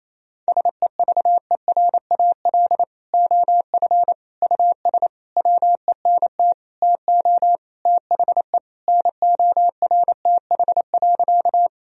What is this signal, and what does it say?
Morse code, heard as a keyed tone, about 28 wpm, 700 Hz. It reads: SEVERAL OF US WENT TO THE NORTH.
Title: Sentences from Top 1000 Words - Review Part 2 - 28wpm